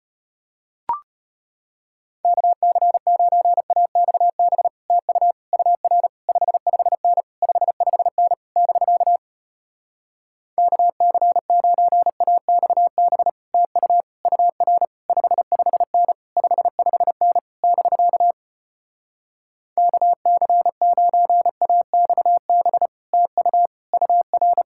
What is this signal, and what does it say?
Morse code, heard as a keyed tone, about 33 wpm, 700 Hz. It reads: KC9AXB TU UR 55N 55N <BK> KC9AXB TU UR 55N 55N <BK> KC9AXB TU UR